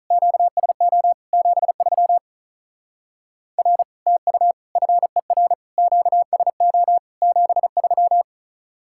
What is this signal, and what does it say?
Morse code, heard as a keyed tone, about 43 wpm, 700 Hz. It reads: QSO 73 R TU FER QSO 73
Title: Speed-Racing Common QSO Phrases - 35wpm